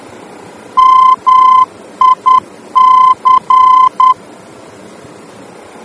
Morse code, heard as a keyed tone, about 10 wpm, 1030 Hz. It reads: MIC
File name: germany / mx / MIC-D-VOR